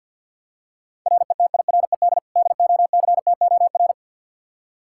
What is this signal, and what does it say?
Morse code, heard as a keyed tone, about 50 wpm, 700 Hz. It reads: RETIRED DOCTOR
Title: Practice QSO - Exchange 3 - More Info - 50wpm